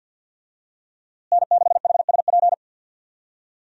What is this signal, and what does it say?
Morse code, heard as a keyed tone, about 52 wpm, 700 Hz. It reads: N6HSP
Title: USA Call Signs - 5 Characters - Repeated 2x - 50wpm